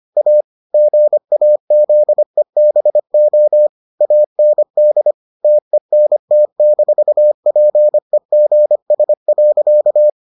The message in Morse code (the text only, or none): A GAZEBO AND TENT-PEGS.